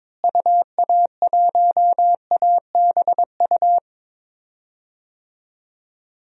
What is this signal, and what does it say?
Morse code, heard as a keyed tone, about 22 wpm, 700 Hz. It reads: UA1ABU
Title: DXCC-100 Call Signs - 6 Characters - Repeated 2x - 22wpm